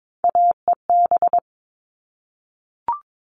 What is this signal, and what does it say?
Morse code, heard as a keyed tone, about 22 wpm, 700 Hz. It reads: AEB
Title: All Cut Numbers - 3-Digit 22wpm